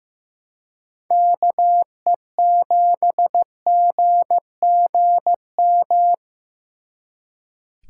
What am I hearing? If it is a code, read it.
Morse code, heard as a keyed tone, about 18 wpm, 700 Hz. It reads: KE7GGM
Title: Speed Racing - USA Call Signs 15wpm